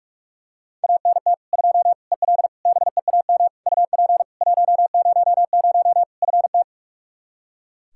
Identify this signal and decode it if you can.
Morse code, heard as a keyed tone, about 42 wpm, 700 Hz. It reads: ANT 2 EL BEAM UP 100 FT
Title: K1USN SST QSO Practice 45wpm